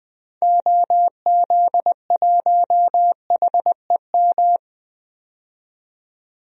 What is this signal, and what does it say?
Morse code, heard as a keyed tone, about 20 wpm, 700 Hz. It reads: OZ1HEM
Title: DXCC-100 Call Signs - 6 Characters - 20wpm